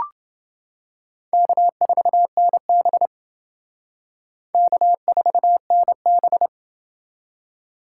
Morse code, heard as a keyed tone, about 28 wpm, 700 Hz. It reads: K4NB K4NB